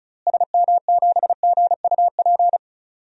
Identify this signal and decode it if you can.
Morse code, heard as a keyed tone, about 36 wpm, 700 Hz. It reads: SM7GUP
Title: DXCC-100 Call Signs - 6 Characters - Repeated 2x - 35wpm